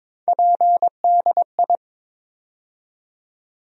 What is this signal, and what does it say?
Morse code, heard as a keyed tone, about 22 wpm, 700 Hz. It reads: PDI
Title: USA Call Sign Postfixes - 3 Letters 22wpm